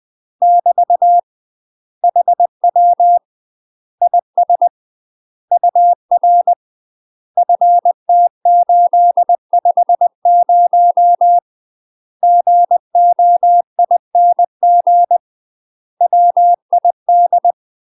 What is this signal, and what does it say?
Morse code, heard as a keyed tone, about 20 wpm, 700 Hz. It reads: <BT> HW IS UR FT850 GOING WID
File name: UK MASTER QSOs TRAINING/20WPM